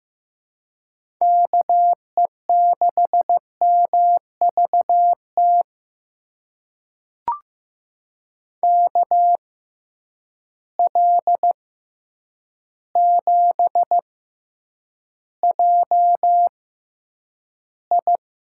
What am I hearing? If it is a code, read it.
Morse code, heard as a keyed tone, about 15 wpm, spaced as for 8 wpm, 700 Hz. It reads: KE6MVT KL7JI
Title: USA Call Signs - 6 Characters - 6x Character Spacing - 15wpm